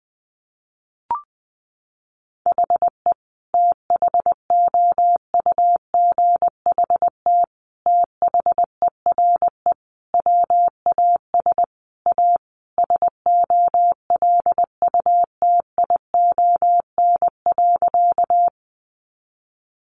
Morse code, heard as a keyed tone, about 20 wpm, 700 Hz. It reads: HE THOUGHT THERE WAS A SOLUTION.
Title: Sentences from Top 1300 Words 20wpm